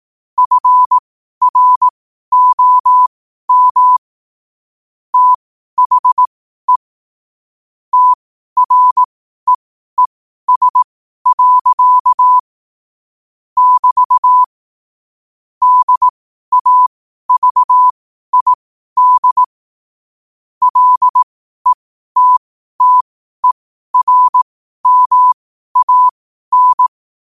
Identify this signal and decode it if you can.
Morse code, heard as a keyed tone, about 18 wpm, spaced as for 12 wpm, 1000 Hz. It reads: FROM THE TREES. <BT> DAVID LETTERMAN